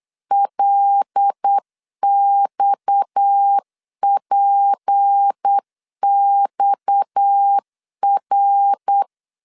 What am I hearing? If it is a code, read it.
Morse code, heard as a keyed tone, about 8 wpm, 800 Hz. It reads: LXPXR
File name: LEZIONI / 6 PKX0 / 6 PKX 8wpm 3041